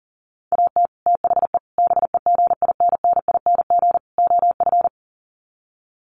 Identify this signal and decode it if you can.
Morse code, heard as a keyed tone, about 37 wpm, 700 Hz. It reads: AT THE BEGINNING OF
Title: Sets of 4 Words 40wpm